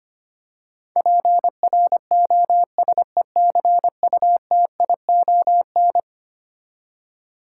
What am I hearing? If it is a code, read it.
Morse code, heard as a keyed tone, about 25 wpm, 700 Hz. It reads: PROSECUTION
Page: morse-code-ninja-practice-25wpm